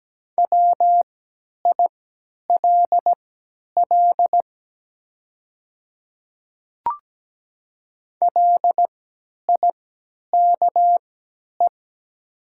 Morse code, heard as a keyed tone, about 17 wpm, spaced as for 10 wpm, 700 Hz. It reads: WILL LIKE
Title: 3x Character Spacing - Top 100 Words - 17wpm